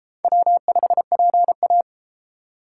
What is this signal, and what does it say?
Morse code, heard as a keyed tone, about 32 wpm, 700 Hz. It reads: W5PA